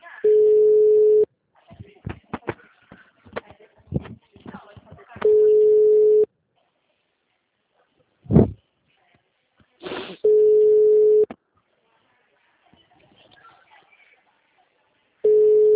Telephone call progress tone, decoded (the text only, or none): ringback tone